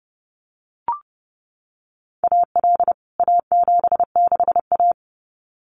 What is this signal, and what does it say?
Morse code, heard as a keyed tone, about 30 wpm, 700 Hz. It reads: AL A76A
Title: CWT Exchange 30wpm